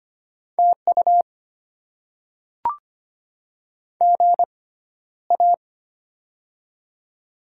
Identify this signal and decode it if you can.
Morse code, heard as a keyed tone, about 25 wpm, spaced as for 13 wpm, 700 Hz. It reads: TU GA
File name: Top 20 QSO Elements - 6x Character Spacing - 25wpm